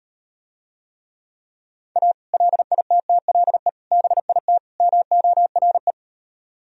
Morse code, heard as a keyed tone, about 38 wpm, 700 Hz. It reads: A LITTLE BIT MORE